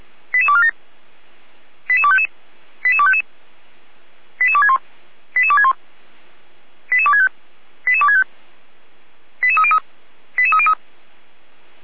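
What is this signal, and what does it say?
Five-tone selective call (ZVEI), heard as quick runs of five tones, tones of about 70 ms. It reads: 88247 88255 88255 88261 88261 88265 88265 88303 88303
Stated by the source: ZVEI five-tone call received on an SDR.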